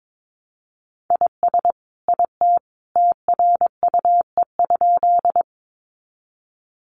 Morse code, heard as a keyed tone, about 22 wpm, 700 Hz. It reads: IS IT TRUE?
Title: Sentences from Top 400 Words 22wpm